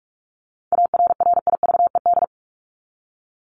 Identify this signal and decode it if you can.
Morse code, heard as a keyed tone, about 44 wpm, 700 Hz. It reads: ARRIVED
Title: ED Suffix - Top 100 Words with Base Words 45wpm